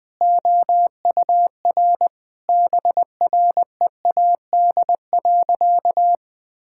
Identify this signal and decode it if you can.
Morse code, heard as a keyed tone, about 20 wpm, 700 Hz. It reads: OUR BREAD.
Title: Sentences from Top 1600 Words 20wpm